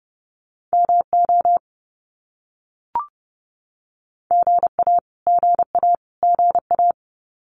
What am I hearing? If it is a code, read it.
Morse code, heard as a keyed tone, about 30 wpm, 700 Hz. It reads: MO GA GA GA
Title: US States Repeated 3x 30wpm